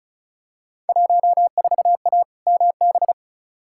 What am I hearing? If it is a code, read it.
Morse code, heard as a keyed tone, about 36 wpm, 690 Hz. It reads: 14A MB
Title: Field Day Exchanges Encore - 35wpm